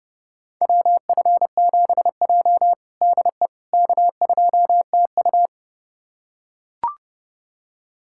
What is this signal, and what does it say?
Morse code, heard as a keyed tone, about 32 wpm, 700 Hz. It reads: WF7J DE K2TU
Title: USA Callsign DE Callsign - 4 Characters - 30wpm